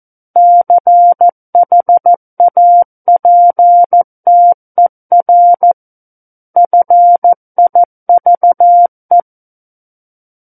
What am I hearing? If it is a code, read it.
Morse code, heard as a keyed tone, about 14 wpm, 700 Hz. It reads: CHAPTER FIVE
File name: Book One / 05-Chapter 5 - 14 WPM